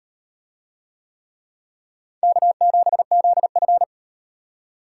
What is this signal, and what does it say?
Morse code, heard as a keyed tone, about 38 wpm, 700 Hz. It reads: K7ZF